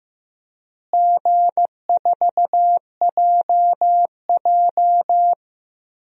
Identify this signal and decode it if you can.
Morse code, heard as a keyed tone, about 15 wpm, 700 Hz. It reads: G4JJ
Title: DXCC-100 Call Signs - 4 Characters - Repeated 3x - 15wpm